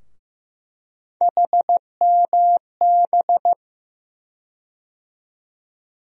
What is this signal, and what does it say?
Morse code, heard as a keyed tone, about 15 wpm, 700 Hz. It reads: HMB
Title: USA Call Sign Postfixes - 3 Letters 15wpm